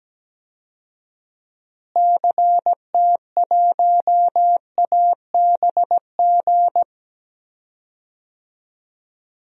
Morse code, heard as a keyed tone, about 17 wpm, 700 Hz. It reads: CT1ABG
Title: DXCC-100 Call Signs - 6 Characters - Repeated 2x - 17wpm